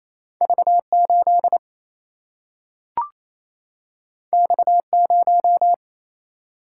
Morse code, heard as a keyed tone, about 28 wpm, 700 Hz. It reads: V8 X0
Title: Single-Letter Single-Number 28wpm